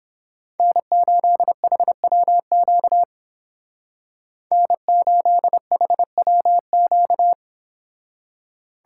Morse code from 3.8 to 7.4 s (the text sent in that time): N8HWQ